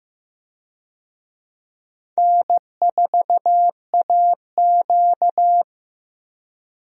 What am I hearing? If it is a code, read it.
Morse code, heard as a keyed tone, about 15 wpm, 700 Hz. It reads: N4AQ
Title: USA Call Signs - 4 Characters - 15wpm